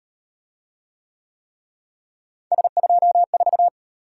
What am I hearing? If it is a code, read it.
Morse code, heard as a keyed tone, about 35 wpm, 700 Hz. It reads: S24